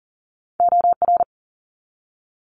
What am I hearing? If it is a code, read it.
Morse code, heard as a keyed tone, about 41 wpm, 700 Hz. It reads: OR
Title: Multi-letter Phonemes 40wpm